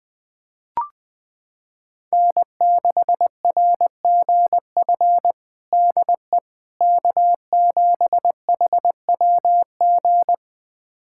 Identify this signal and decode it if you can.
Morse code, heard as a keyed tone, about 20 wpm, 700 Hz. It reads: N6RGF DE K7HWG
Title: USA Callsign DE Callsign - 5 Characters - 20wpm